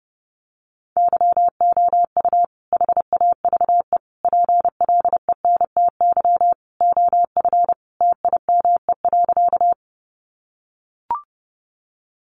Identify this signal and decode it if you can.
Morse code, heard as a keyed tone, about 30 wpm, 700 Hz. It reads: YOU HAVE PLENTY OF TIME.
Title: Sentences from Top 1300 Words 30wpm